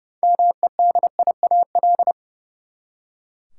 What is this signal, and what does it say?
Morse code, heard as a keyed tone, about 32 wpm, 700 Hz. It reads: MEDIAL
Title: Practice QSO - Exchange 3 - More Info - 30wpm